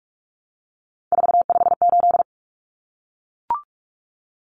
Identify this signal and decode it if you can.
Morse code, heard as a keyed tone, about 44 wpm, 700 Hz. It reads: 458
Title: Numbers - 3-Digit 45wpm